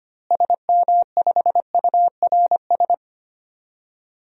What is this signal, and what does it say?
Morse code, heard as a keyed tone, about 25 wpm, 700 Hz. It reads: SM5URS
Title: DXCC-100 Call Signs - 6 Characters - Repeated 3x - 25wpm